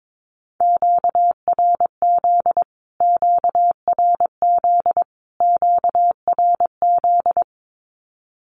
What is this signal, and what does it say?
Morse code, heard as a keyed tone, about 22 wpm, 700 Hz. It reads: QRZ QRZ QRZ
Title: Q-Codes Repeated 3x 22wpm